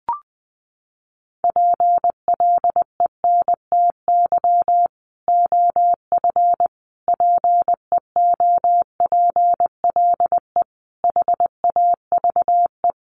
Morse code, heard as a keyed tone, about 20 wpm, 700 Hz. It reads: PLENTY OF PEOPLE HAVE
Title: Sentences from Top 1300 Words 20wpm